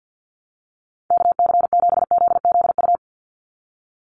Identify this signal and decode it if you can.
Morse code, heard as a keyed tone, about 50 wpm, 700 Hz. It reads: KC7ZZU
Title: USA Call Signs 50wpm